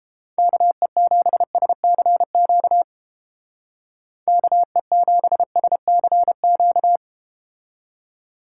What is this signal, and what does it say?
Morse code, heard as a keyed tone, about 31 wpm, 700 Hz. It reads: KE7SCQ KE7SCQ